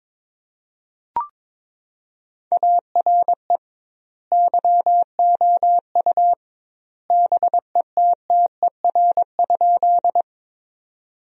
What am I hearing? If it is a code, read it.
Morse code, heard as a keyed tone, about 22 wpm, 700 Hz. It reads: ARE YOU BETTER?
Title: 2x Word-Spacing - Sentences from Top 400 Words - 22wpm